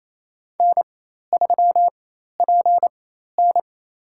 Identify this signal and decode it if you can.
Morse code, heard as a keyed tone, about 28 wpm, spaced as for 13 wpm, 700 Hz. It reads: N3PN
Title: USA Call Signs - 4 Characters - 4x Character Spacing - 28wpm